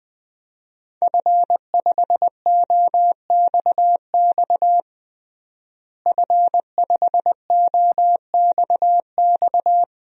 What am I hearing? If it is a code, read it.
Morse code, heard as a keyed tone, about 21 wpm, 700 Hz. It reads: F5OXX F5OXX
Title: DXCC-100 Call Signs - 5 Characters - Repeated 2x - 20wpm